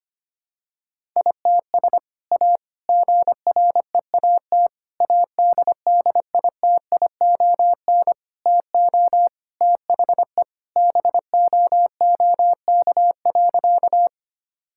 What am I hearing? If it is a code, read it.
Morse code, heard as a keyed tone, about 25 wpm, 700 Hz. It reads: ITS A GREAT ADDITION TO THE BOOK.
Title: Sentences from Top 1000 Words - Review Part 6 - 25wpm